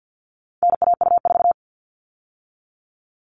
Morse code, heard as a keyed tone, about 50 wpm, 700 Hz. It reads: NAU4
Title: All Cut Numbers - 4-Digit 50wpm